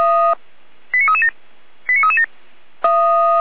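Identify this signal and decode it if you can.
Five-tone selective call (ZVEI), heard as quick runs of five tones, tones of about 70 ms. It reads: 89337 89337
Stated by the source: Received on SDR, ZVEI five-tone call.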